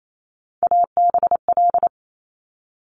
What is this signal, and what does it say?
Morse code, heard as a keyed tone, about 28 wpm, 700 Hz. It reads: ABL
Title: Top Three Letter Combinations in Words 28wpm